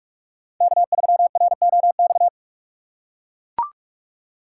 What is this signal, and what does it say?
Morse code, heard as a keyed tone, about 43 wpm, 700 Hz. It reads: K3ROX
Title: USA Call Signs - 5 Characters - 3x Character Spacing - 45wpm